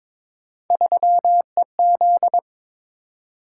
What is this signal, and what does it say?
Morse code, heard as a keyed tone, about 23 wpm, 700 Hz. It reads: 3EZ
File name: practice / 22wpm / Three Letter-Number Combinations - 22wpm